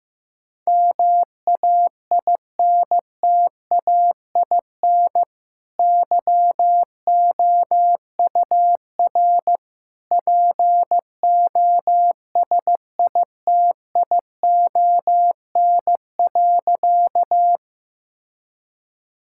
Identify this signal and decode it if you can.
Morse code, heard as a keyed tone, about 16 wpm, 700 Hz. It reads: MAINTAIN YOUR POSITION.
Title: Sentences from Top 1000 Words - Review Part 5 - 15wpm